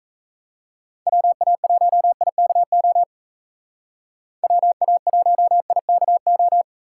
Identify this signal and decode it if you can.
Morse code, heard as a keyed tone, about 34 wpm, 700 Hz. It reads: WA1IKO WA1IKO
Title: Speed Racing - USA 2x3 Call Signs 28wpm